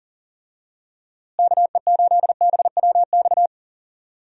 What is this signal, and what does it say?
Morse code, heard as a keyed tone, about 40 wpm, 700 Hz. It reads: KE8BWX
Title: USA Call Signs - 6 Characters - Repeated 3x - 40wpm